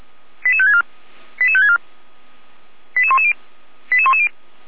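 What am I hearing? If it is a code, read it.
Five-tone selective call (ZVEI), heard as quick runs of five tones, tones of about 70 ms. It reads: 88564 88564 88110 88110
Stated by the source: ZVEI five-tone selective call 88355 received by SDR.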